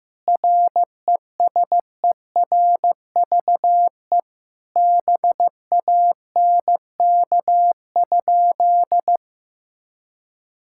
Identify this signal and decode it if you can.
Morse code, heard as a keyed tone, about 15 wpm, 700 Hz. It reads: RESERVE BANK?
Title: Sentences from Top 1000 Words - Review Part 6 - 15wpm